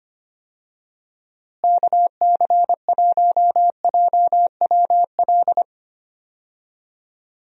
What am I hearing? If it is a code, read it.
Morse code, heard as a keyed tone, about 25 wpm, 700 Hz. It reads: KC1JWL
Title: USA Call Signs - 6 Characters - Repeated 2x - 25wpm